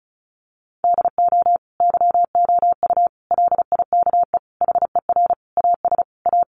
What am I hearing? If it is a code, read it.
Morse code, heard as a keyed tone, about 34 wpm, 700 Hz. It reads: DO YOU LIKE HER AS A
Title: Sentences from Top 1100 Words 35wpm